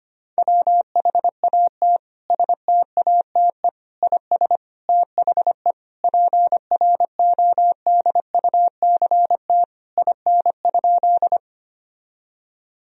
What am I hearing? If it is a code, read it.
Morse code, heard as a keyed tone, about 25 wpm, 700 Hz. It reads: WHAT STATE IS THE PRODUCT IN?
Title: Sentences from Top 1000 Words - Review Part 2 - 25wpm